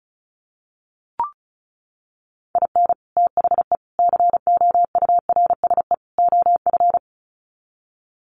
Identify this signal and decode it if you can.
Morse code, heard as a keyed tone, about 34 wpm, 700 Hz. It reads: IN THE COURSE OF
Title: Sets of 4 Words 35wpm